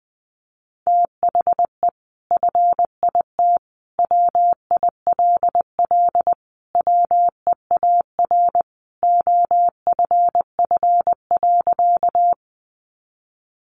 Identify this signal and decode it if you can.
Morse code, heard as a keyed tone, about 20 wpm, 700 Hz. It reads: THE FIT WILL WEAR OFF.
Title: Sentences from Top 1500 Words 20wpm